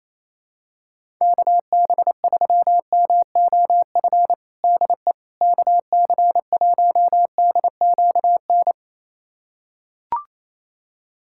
Morse code, heard as a keyed tone, about 28 wpm, 700 Hz. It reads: KB3MOF DE KC1DQN
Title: USA Callsign DE Callsign 28wpm